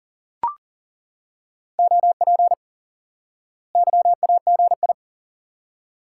Morse code, heard as a keyed tone, about 40 wpm, 700 Hz. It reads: OP YAGI
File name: IQE Course 3 - Lesson 89 - YAGI - Two Elements - 40wpm